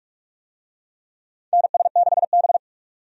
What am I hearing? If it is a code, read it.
Morse code, heard as a keyed tone, about 46 wpm, 700 Hz. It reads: NS6B
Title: USA Call Signs - 4 Characters - 45wpm